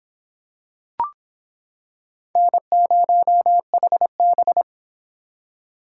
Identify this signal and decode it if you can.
Morse code, heard as a keyed tone, about 26 wpm, 700 Hz. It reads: N0HB